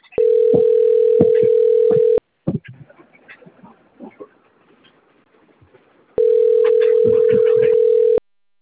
Ringback tone (North America), 440 + 480 Hz.